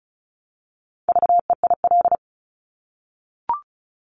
Morse code, heard as a keyed tone, about 36 wpm, 700 Hz. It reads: VEIL
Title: Multi-letter Phonemes Repeated 3x 35wpm